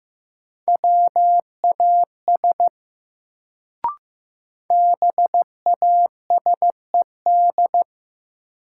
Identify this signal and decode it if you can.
Morse code, heard as a keyed tone, about 15 wpm, 700 Hz. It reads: WAS BASED
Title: Sets of 2 Words - New Word Drill - 15wpm